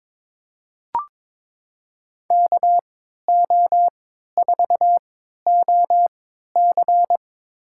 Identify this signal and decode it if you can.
Morse code, heard as a keyed tone, about 22 wpm, spaced as for 13 wpm, 700 Hz. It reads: KO4OC
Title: USA Call Signs - 3x Character Spacing - 22wpm